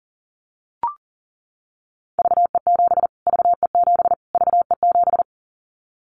Morse code, heard as a keyed tone, about 37 wpm, 700 Hz. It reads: VE7 VE7 VE7
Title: Callsign Prefixes for Top 25 DXCC - Repeated 3x - 40wpm